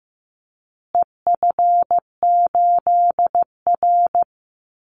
Morse code, heard as a keyed tone, about 15 wpm, 700 Hz. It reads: EF8R
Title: Call Signs Repeated 2x - 15wpm